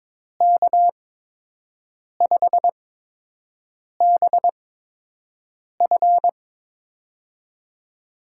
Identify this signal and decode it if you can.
Morse code, heard as a keyed tone, about 23 wpm, spaced as for 6 wpm, 700 Hz. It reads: K5BF
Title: USA Call Signs - 5 Characters - 8x Character Spacing - 22wpm